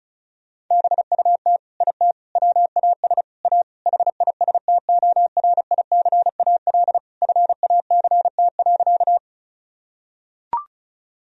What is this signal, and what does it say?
Morse code, heard as a keyed tone, about 36 wpm, 700 Hz. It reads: BUT IT WAS A HISTORICAL FACT.
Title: Sentences from Top 1000 Words - Review Part 4 - 35wpm